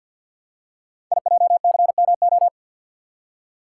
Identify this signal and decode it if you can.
Morse code, heard as a keyed tone, about 50 wpm, 690 Hz. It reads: IJCMO
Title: Code Groups - 5 Letters - 50wpm